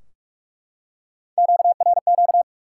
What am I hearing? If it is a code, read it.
Morse code, heard as a keyed tone, about 43 wpm, 700 Hz. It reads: QRQ
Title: Top 180 QSO Elements 45wpm